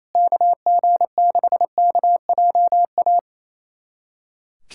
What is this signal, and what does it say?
Morse code, heard as a keyed tone, about 28 wpm, 700 Hz. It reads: KG6KJA
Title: USA Callsign DE Callsign - 6 Characters - 28wpm